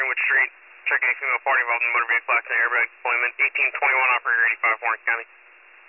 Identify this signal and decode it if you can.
Emergency Alert System (SAME) tones, heard as harsh buzzing bursts, with speech. It is heard from 0.8 to 4.8 s.